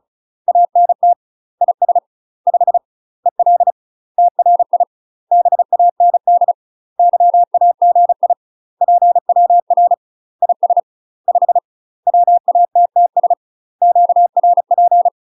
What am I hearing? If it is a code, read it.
Morse code, heard as a keyed tone, about 34 wpm, 700 Hz. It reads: ANT IS 5 EL TRI BAND YAGI PWR IS 5 WATTS QRP
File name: UK MASTER QSOs TRAINING / 35WPM / UK MASTER QSOs 0029